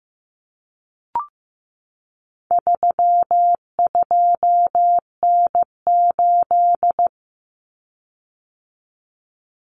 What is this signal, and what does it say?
Morse code, heard as a keyed tone, about 15 wpm, 700 Hz. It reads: 32N8
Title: Common Cut Numbers - 4-Digit 15wpm